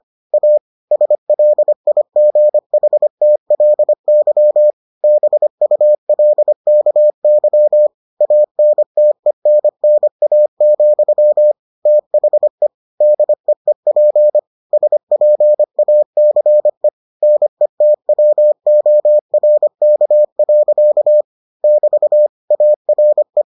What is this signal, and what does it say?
Morse code, heard as a keyed tone, about 25 wpm, 600 Hz. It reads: A SLIGHTLY BULKY ANTENNA, THE DEEP SPACE NETWORK. <BT> ARE